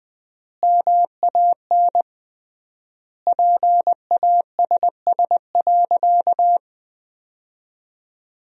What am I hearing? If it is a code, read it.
Morse code, heard as a keyed tone, about 20 wpm, 700 Hz. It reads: MAN PASS.
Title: 3x Word-Spacing - Sentences from Top 400 Words - 20wpm